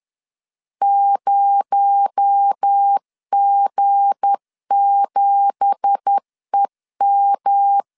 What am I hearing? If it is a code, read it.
Morse code, heard as a keyed tone, about 10 wpm, 800 Hz. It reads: 0G7EM
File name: CW / DET / 1 9 1500C S A 10wpm 4511